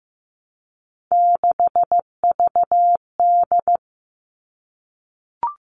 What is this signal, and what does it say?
Morse code, heard as a keyed tone, about 15 wpm, 700 Hz. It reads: 6VD
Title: All Cut Numbers - 3-Digit 15wpm